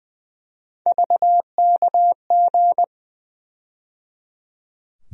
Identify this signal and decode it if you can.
Morse code, heard as a keyed tone, about 20 wpm, 700 Hz. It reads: VKG
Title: USA Call Sign Postfixes - 3 Letters 20wpm